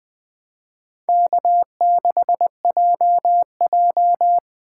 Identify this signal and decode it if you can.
Morse code, heard as a keyed tone, about 20 wpm, 700 Hz. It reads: K6JJ